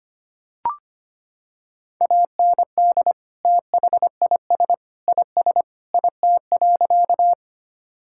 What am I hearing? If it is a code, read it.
Morse code, heard as a keyed tone, about 25 wpm, 700 Hz. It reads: AND THIS IS IT.
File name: practice / 25wpm / Sentences with 3-Word Phrases - 25wpm